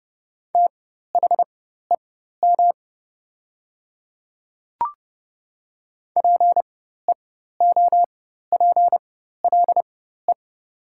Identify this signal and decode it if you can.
Morse code, heard as a keyed tone, about 30 wpm, spaced as for 14 wpm, 700 Hz. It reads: THEM PEOPLE
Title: 4x Character Spacing - Top 200 Words - 30wpm